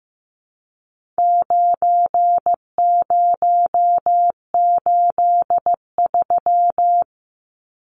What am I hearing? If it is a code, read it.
Morse code, heard as a keyed tone, about 15 wpm, 700 Hz. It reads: 9083
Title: Numbers - 4-Digit 15wpm